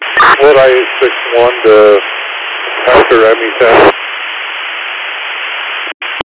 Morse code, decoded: none

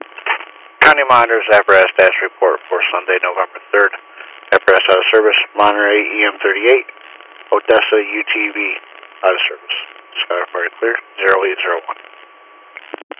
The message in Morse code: none

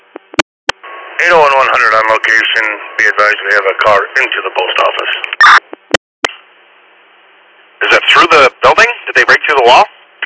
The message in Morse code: none

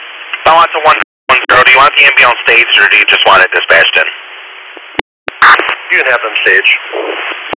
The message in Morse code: none

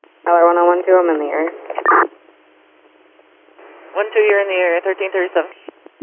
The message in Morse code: none

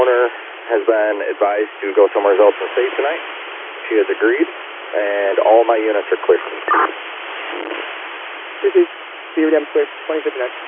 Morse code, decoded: none

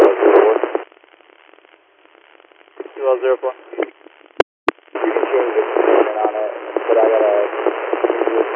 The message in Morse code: none

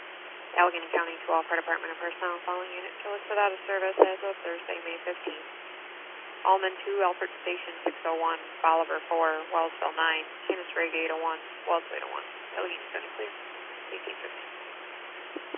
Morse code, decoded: none

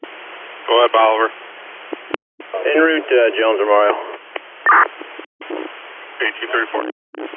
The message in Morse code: none

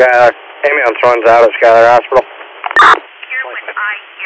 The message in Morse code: none

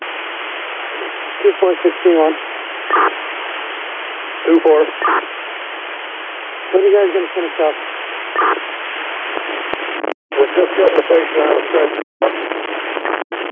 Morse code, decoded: none

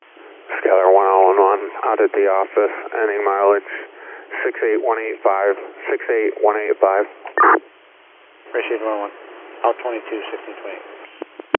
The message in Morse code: none

